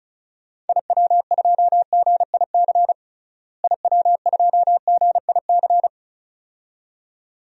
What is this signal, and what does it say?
Morse code, heard as a keyed tone, about 36 wpm, 700 Hz. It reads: IW2GIC IW2GIC
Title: DXCC-100 Call Signs - 6 Characters - Repeated 2x - 35wpm